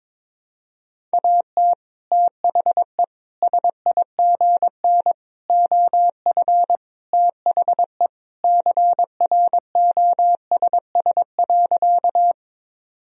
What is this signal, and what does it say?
Morse code, heard as a keyed tone, about 22 wpm, 700 Hz. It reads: AT THE SIGN OF THE CROSS.
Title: Sentences from Top 1000 Words - Review Part 1 - 22wpm